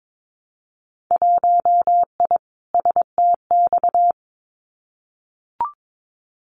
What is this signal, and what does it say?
Morse code, heard as a keyed tone, about 22 wpm, 700 Hz. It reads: 1I STX
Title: Winter Field Day Exchanges 22wpm